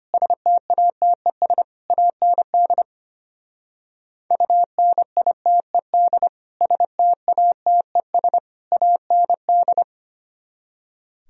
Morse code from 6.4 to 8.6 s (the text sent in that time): STATES